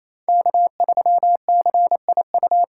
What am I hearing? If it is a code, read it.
Morse code, heard as a keyed tone, about 28 wpm, 700 Hz. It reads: K3CIU
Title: USA Callsign DE Callsign - 5 Characters - 28wpm